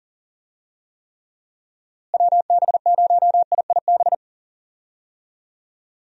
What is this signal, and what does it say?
Morse code, heard as a keyed tone, about 40 wpm, 700 Hz. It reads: WB0IIB
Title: USA Call Signs - 6 Characters - Repeated 3x - 40wpm